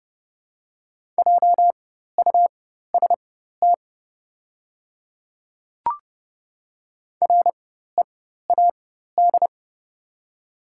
Morse code, heard as a keyed tone, about 32 wpm, spaced as for 14 wpm, 700 Hz. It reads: JUST READ
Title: 4x Character Spacing - Top 300 Words - 30wpm